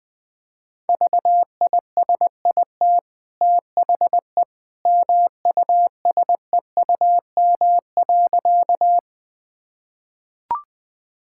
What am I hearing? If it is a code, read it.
Morse code, heard as a keyed tone, about 20 wpm, 700 Hz. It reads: VISIT THE MUSEUM.
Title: Easy Sentences with Hints - 20wpm